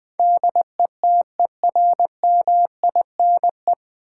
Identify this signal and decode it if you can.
Morse code, heard as a keyed tone, about 20 wpm, 700 Hz. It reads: DETERMINE
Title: Sentences from Top 700 Words - New Word Drill - 20wpm